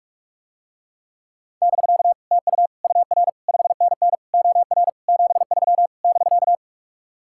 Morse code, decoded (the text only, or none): <BK> TU UR 5NN OR 73 <BK>